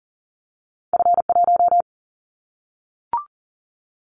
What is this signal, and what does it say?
Morse code, heard as a keyed tone, about 41 wpm, 700 Hz. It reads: F1
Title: Callsign Prefixes for Top 50 DXCC - 40wpm